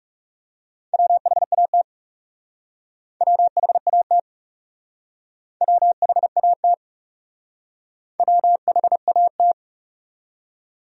Morse code, heard as a keyed tone, about 39 wpm, 700 Hz. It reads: WHAT WHAT WHAT WHAT